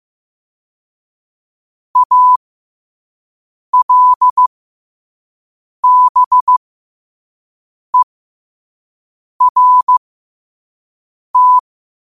Morse code, heard as a keyed tone, about 15 wpm, spaced as for 5 wpm, 1000 Hz. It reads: ALBERT
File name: 5wpm